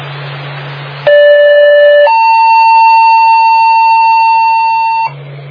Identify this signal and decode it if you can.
Two-tone page, heard as a one-second tone, then a three-second tone, about 580 then 900 Hz. It is heard from 1.1 to 5.1 s.